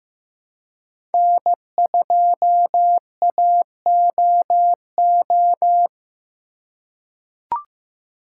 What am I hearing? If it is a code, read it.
Morse code, heard as a keyed tone, about 16 wpm, 700 Hz. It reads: N2AOO
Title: USA Call Signs - 5 Characters - 3x Character Spacing - 15wpm